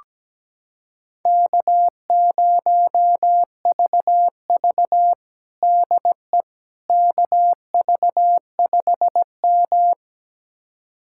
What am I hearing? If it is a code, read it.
Morse code, heard as a keyed tone, about 17 wpm, 700 Hz. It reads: K0VV DE KV5M
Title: USA Callsign DE Callsign - 4 Characters - 17wpm